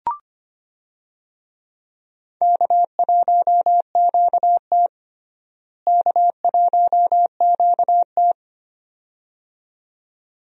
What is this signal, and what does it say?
Morse code, heard as a keyed tone, about 25 wpm, 700 Hz. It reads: K1QT K1QT
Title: USA Call Signs - 4 Characters - Repeated 2x - 25wpm